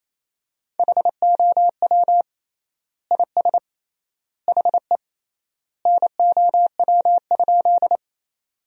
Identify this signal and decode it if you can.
Morse code, heard as a keyed tone, about 28 wpm, 700 Hz. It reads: HOW IS HE NOW?
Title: 3x Word-Spacing - Sentences from Top 100 Words - 28wpm